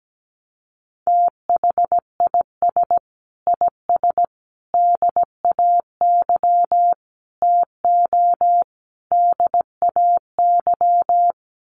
Morse code, heard as a keyed tone, about 17 wpm, 700 Hz. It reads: THIS IS DAY TO DAY
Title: Sets of 5 Words Encore 17wpm